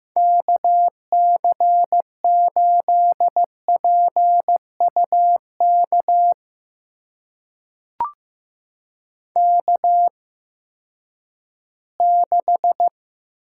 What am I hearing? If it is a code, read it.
Morse code, heard as a keyed tone, about 15 wpm, spaced as for 10 wpm, 700 Hz. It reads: KC8PUK K6
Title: USA Call Signs - 8x Character Spacing - 15wpm